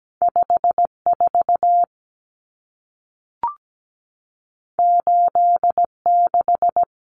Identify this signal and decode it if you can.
Morse code, heard as a keyed tone, about 17 wpm, 700 Hz. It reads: 54 86
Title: Numbers - 2-Digit 17wpm